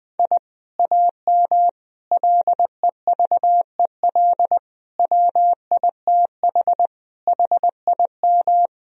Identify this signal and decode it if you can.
Morse code, heard as a keyed tone, about 20 wpm, 700 Hz. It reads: I AM LEVEL WITH HIM
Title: Sentences from Top 1000 Words - Review Part 4 - 20wpm